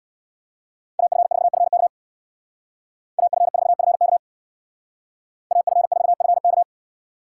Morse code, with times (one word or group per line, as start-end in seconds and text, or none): 1.0-1.9 AF4FK
3.2-4.2 AF4FK
5.5-6.6 AF4FK